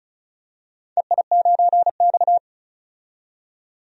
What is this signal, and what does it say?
Morse code, heard as a keyed tone, about 36 wpm, 700 Hz. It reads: EI9X
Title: DXCC-100 Call Signs - 4 Characters - 35wpm